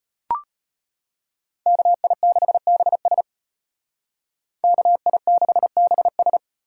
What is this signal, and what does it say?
Morse code, heard as a keyed tone, about 32 wpm, 700 Hz. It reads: KI6BS KI6BS